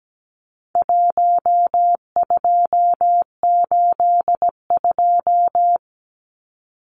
Morse code, heard as a keyed tone, about 17 wpm, 700 Hz. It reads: 1282
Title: Numbers - 4-Digit 17wpm